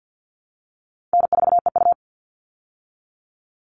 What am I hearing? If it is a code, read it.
Morse code, heard as a keyed tone, about 50 wpm, 700 Hz. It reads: N4EU
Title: All Cut Numbers - 4-Digit 50wpm